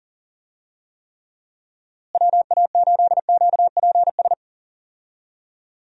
Morse code, heard as a keyed tone, about 37 wpm, 700 Hz. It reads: WA8QPS
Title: USA Call Signs - 6 Characters - Repeated 3x - 40wpm